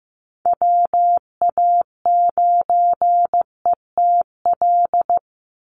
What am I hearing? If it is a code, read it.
Morse code, heard as a keyed tone, about 15 wpm, 700 Hz. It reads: WA9ETL
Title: USA Call Signs 15wpm